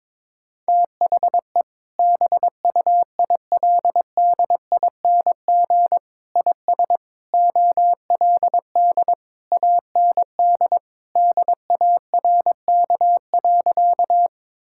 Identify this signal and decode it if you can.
Morse code, heard as a keyed tone, about 22 wpm, 700 Hz. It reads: THE BUILDING IS OLD AND DARK.
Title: Sentences from Top 1000 Words - Review Part 5 - 22wpm